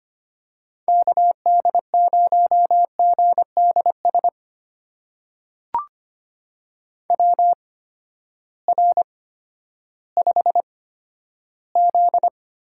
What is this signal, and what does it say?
Morse code, heard as a keyed tone, about 25 wpm, spaced as for 12 wpm, 700 Hz. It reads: KD0GDS WR5Z
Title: USA Call Signs - 8x Character Spacing - 25wpm